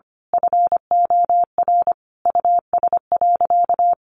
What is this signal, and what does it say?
Morse code, heard as a keyed tone, about 25 wpm, 700 Hz. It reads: FOR US.
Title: Sentences from Top 1200 Words 25wpm